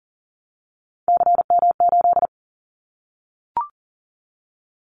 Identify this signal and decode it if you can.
Morse code, heard as a keyed tone, about 40 wpm, 700 Hz. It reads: CM8
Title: Callsign Prefixes for Top 100 DXCC - 40wpm